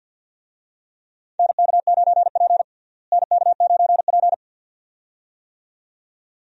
Morse code, heard as a keyed tone, about 50 wpm, 700 Hz. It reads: NK9P NK9P
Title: USA Call Signs - 4 Characters - Repeated 2x - 50wpm